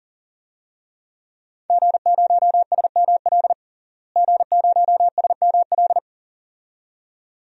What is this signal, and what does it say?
Morse code, heard as a keyed tone, about 40 wpm, 700 Hz. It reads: G0SML G0SML
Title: DXCC-100 Call Signs - 5 Characters - Repeated 2x - 40wpm